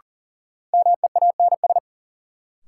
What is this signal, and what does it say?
Morse code, heard as a keyed tone, about 40 wpm, 700 Hz. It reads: MEANS
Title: Sets of 5 Words - New Word Drill - 40wpm